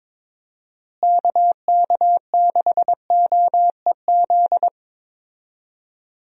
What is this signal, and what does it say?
Morse code, heard as a keyed tone, about 21 wpm, 700 Hz. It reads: KK6OEZ
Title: USA Call Signs - 6 Characters - Repeated 3x - 22wpm